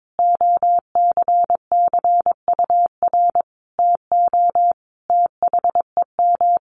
Morse code, heard as a keyed tone, about 22 wpm, 700 Hz. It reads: OCCUR TO THEM
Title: Sentences from Top 900 Words 22wpm